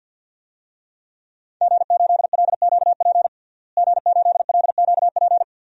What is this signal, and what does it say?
Morse code, heard as a keyed tone, about 50 wpm, 700 Hz. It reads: G8LQP G8LQP
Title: DXCC-100 Call Signs - 5 Characters - Repeated 2x - 50wpm